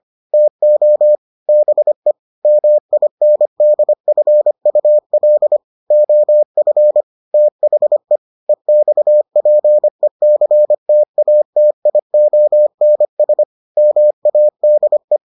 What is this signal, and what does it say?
Morse code, heard as a keyed tone, about 25 wpm, 600 Hz. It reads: TO BE MINDFUL OF THE EXPECTATIONS MADE